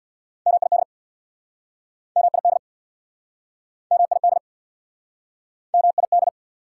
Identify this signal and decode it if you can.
Morse code, heard as a keyed tone, about 69 wpm, 690 Hz. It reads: MID MID MID MID